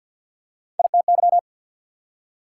Morse code, heard as a keyed tone, about 48 wpm, 690 Hz. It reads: ITY
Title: Prefixes and Suffixes 50wpm